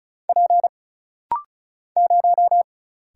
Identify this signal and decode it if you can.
Morse code, heard as a keyed tone, about 36 wpm, 700 Hz. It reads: P 0
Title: Single Letter-Number - Rapid-Fire 35wpm